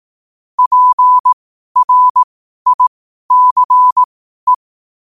Morse code, heard as a keyed tone, about 18 wpm, spaced as for 12 wpm, 1000 Hz. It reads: PRICE
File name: qotdfiles / 12wpm